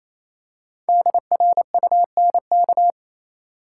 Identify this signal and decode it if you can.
Morse code, heard as a keyed tone, about 28 wpm, 700 Hz. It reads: DRUNK
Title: Five Letter Words - 28wpm